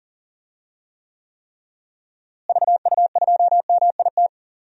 Morse code, heard as a keyed tone, about 40 wpm, 700 Hz. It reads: VU2MIT
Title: DXCC-100 Call Signs - 6 Characters - Repeated 3x - 40wpm